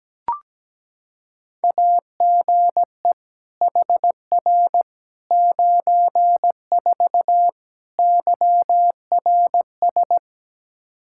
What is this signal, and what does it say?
Morse code, heard as a keyed tone, about 17 wpm, 700 Hz. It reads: AGE HR 94 YRS
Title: Practice QSO - Exchange 3 - More Info - 17wpm